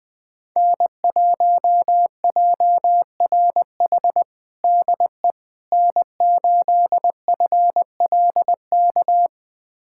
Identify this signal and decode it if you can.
Morse code, heard as a keyed tone, about 20 wpm, 700 Hz. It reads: N1JRH DE N8FLK
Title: USA Callsign DE Callsign - 5 Characters - 20wpm